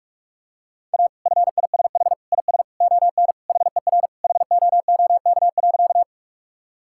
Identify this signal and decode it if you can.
Morse code, heard as a keyed tone, about 46 wpm, 700 Hz. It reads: A FISH IS ON HER HOOK.
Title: Sentences from Top 1600 Words 45wpm